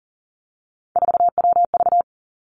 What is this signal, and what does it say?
Morse code, heard as a keyed tone, about 42 wpm, 700 Hz. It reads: 4WV
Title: Three Challenging Letter-Number Combinations 40wpm